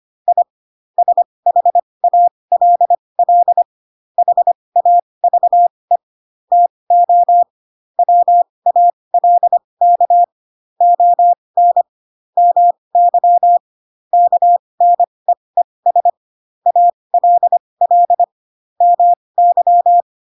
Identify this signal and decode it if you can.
Morse code, heard as a keyed tone, about 25 wpm, spaced as for 20 wpm, 700 Hz. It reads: I SHALL HAVE TO WALK ON MY KNEES ALL MY
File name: Pinocchio / Pinocchio ch7 15 25 ik1hge